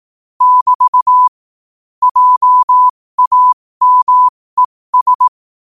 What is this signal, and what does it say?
Morse code, heard as a keyed tone, about 18 wpm, spaced as for 15 wpm, 1000 Hz. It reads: <BT> JAMES
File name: qotdfiles / 15wpm